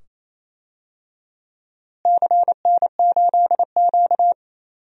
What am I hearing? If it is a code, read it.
Morse code, heard as a keyed tone, about 28 wpm, 700 Hz. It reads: CN8Q
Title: DXCC-100 Call Signs - 4 Characters - Repeated 2x - 28wpm